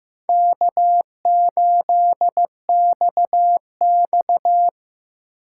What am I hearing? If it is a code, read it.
Morse code, heard as a keyed tone, about 15 wpm, 700 Hz. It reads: K8XX
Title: USA Call Signs - 4 Characters - 2x Character Spacing - 15wpm